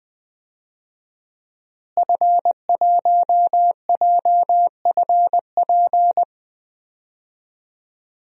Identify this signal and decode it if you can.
Morse code, heard as a keyed tone, about 20 wpm, 700 Hz. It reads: F1JFP
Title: DXCC-100 Call Signs - 5 Characters - Repeated 2x - 20wpm